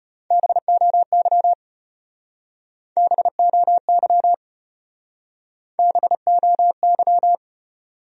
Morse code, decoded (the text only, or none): BOY BOY BOY